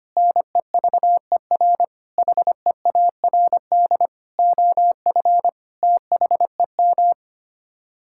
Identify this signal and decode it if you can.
Morse code, heard as a keyed tone, about 25 wpm, 700 Hz. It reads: NEVER HEARD OF THEM
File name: Sets of 9 Words - 25wpm